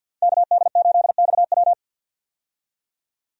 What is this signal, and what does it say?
Morse code, heard as a keyed tone, about 50 wpm, 700 Hz. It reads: KD8XW
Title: USA Call Signs - 5 Characters - 50wpm